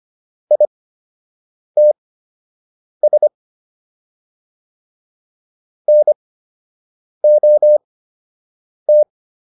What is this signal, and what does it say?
Morse code, heard as a keyed tone, about 25 wpm, spaced as for 7 wpm, 600 Hz. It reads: ITS NOT